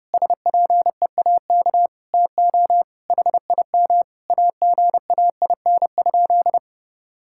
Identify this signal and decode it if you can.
Morse code, heard as a keyed tone, about 30 wpm, 700 Hz. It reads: SPEAK TO HIM AGAIN?
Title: Sentences from Top 1600 Words 30wpm